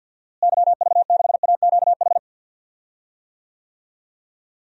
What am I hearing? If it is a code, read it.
Morse code, heard as a keyed tone, about 47 wpm, 700 Hz. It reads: YV6AQH
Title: DXCC-100 Call Signs - 6 Characters - Repeated 2x - 50wpm